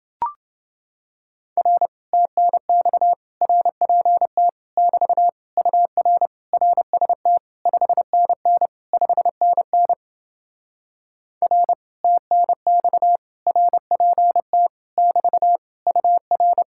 Morse code, heard as a keyed tone, about 30 wpm, 700 Hz. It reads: R TNX RPT <BT> UR RST 5NN 5NN R TNX RPT <BT> UR